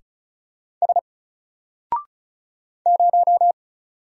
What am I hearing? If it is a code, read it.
Morse code, heard as a keyed tone, about 36 wpm, 700 Hz. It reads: S 0
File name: Single-Letter-Number-35wpm